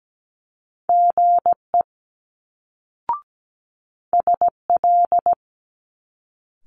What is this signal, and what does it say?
Morse code, heard as a keyed tone, about 17 wpm, 700 Hz. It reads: GE SL
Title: Multi-letter Phonemes Quick Fire 17wpm